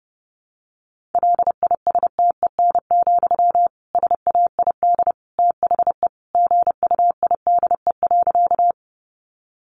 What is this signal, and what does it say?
Morse code, heard as a keyed tone, about 30 wpm, 700 Hz. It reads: LISTEN, SAID THE GUIDE.
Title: Sentences from Top 600 Words 30wpm